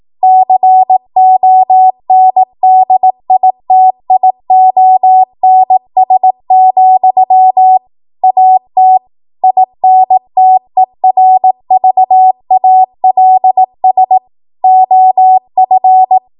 Morse code, heard as a keyed tone, about 18 wpm, 750 Hz. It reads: CONDITIONS, AT INTERVALS OF
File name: Archive / 18 WPM / 18WPM